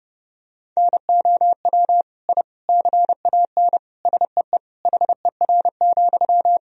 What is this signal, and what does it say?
Morse code, heard as a keyed tone, about 32 wpm, 700 Hz. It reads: NOW I CAN SEE HER,
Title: Sentences from Top 1000 Words - Review Part 3 - 30wpm